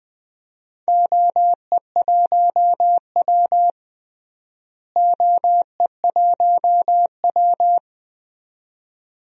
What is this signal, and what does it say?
Morse code, heard as a keyed tone, about 21 wpm, 700 Hz. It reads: OE1W OE1W
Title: DXCC-100 Call Signs - 4 Characters - Repeated 2x - 20wpm